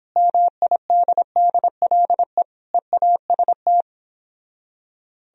Morse code, heard as a keyed tone, about 30 wpm, 700 Hz. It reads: MIDDLE EAST